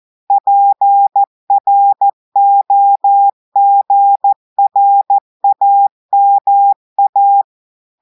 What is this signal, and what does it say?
Morse code, heard as a keyed tone, about 14 wpm, 800 Hz. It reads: PROGRAMA
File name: audios / portugais / portugais texte en clair 2 vitesse 14 mots minutes